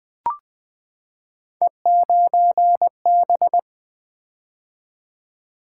Morse code, heard as a keyed tone, about 20 wpm, 700 Hz. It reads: E9B
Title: Three Letter-Number Combinations - 20wpm